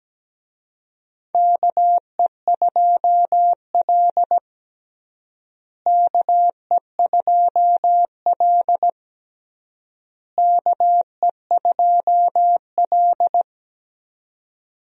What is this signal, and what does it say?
Morse code, heard as a keyed tone, about 17 wpm, 700 Hz. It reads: KE2L KE2L KE2L
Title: USA Call Signs - 4 Characters - Repeated 3x - 17wpm